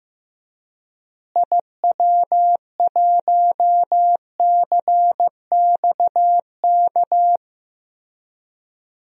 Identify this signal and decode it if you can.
Morse code, heard as a keyed tone, about 15 wpm, 700 Hz. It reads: IW1CXK
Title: DXCC-100 Call Signs - 6 Characters - Repeated 2x - 15wpm